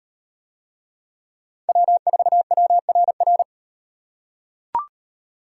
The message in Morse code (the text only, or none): W4WRR